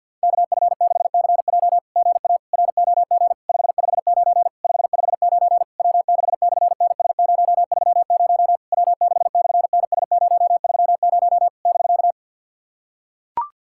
Morse code, heard as a keyed tone, about 48 wpm, 690 Hz. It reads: KF6CJ GA ROG 559 559 W6/NS030 W6/NS030 <BK>
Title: SOTA QSO Practice 50wpm